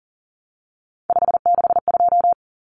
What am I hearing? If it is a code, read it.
Morse code, heard as a keyed tone, about 40 wpm, 700 Hz. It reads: 562
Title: Numbers - 3-Digit 40wpm